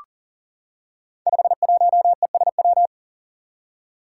Morse code, heard as a keyed tone, about 40 wpm, 700 Hz. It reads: 51ESW
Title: Five Challenging Letter-Number Combinations 40wpm